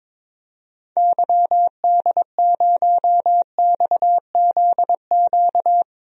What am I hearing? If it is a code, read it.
Morse code, heard as a keyed tone, about 22 wpm, 700 Hz. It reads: YD0XZQ
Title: DXCC-100 Call Signs - 6 Characters - 22wpm